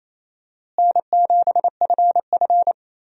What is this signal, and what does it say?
Morse code, heard as a keyed tone, about 28 wpm, 700 Hz. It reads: N7FF
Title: USA Call Signs - 4 Characters - Repeated 2x - 28wpm